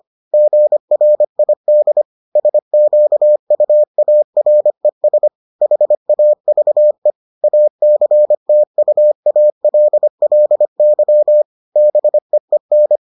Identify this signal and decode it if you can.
Morse code, heard as a keyed tone, about 25 wpm, 600 Hz. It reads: GRID SQUARES HAVE ACTUALLY BEEN